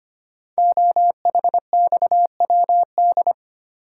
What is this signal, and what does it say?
Morse code, heard as a keyed tone, about 25 wpm, 700 Hz. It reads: OHXWD
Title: Code Groups - 5 Letters - 25wpm